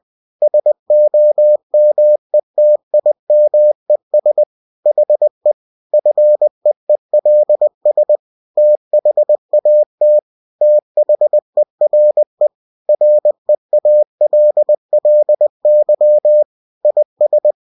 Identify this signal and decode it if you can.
Morse code, heard as a keyed tone, about 20 wpm, 600 Hz. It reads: SOMETIMES HE FEELS THAT THERE REALLY IS
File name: cw-books / winnie-the-pooh / winnie-20wpm-page-0000